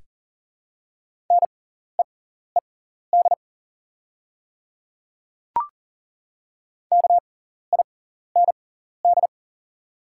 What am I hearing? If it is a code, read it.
Morse code, heard as a keyed tone, about 37 wpm, spaced as for 14 wpm, 690 Hz. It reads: NEED KIND
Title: 6x Character Spacing - Top 200 Words - 40wpm